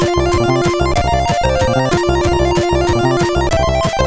There is no code.